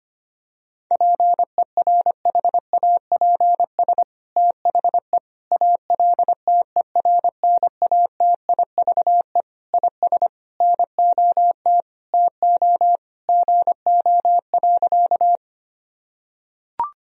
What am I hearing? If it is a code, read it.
Morse code, heard as a keyed tone, about 25 wpm, 700 Hz. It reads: PERHAPS THE ALTERNATIVE IS NOT TO GO.
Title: Sentences from Top 1000 Words - Review Part 4 - 25wpm